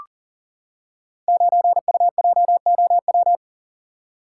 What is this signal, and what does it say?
Morse code, heard as a keyed tone, about 40 wpm, 700 Hz. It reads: 9UJOW
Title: Code Groups - 5 Letter:Numbers - 40wpm